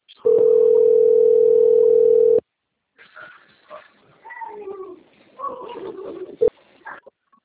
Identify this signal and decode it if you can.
Ringback tone (North America), 440 + 480 Hz.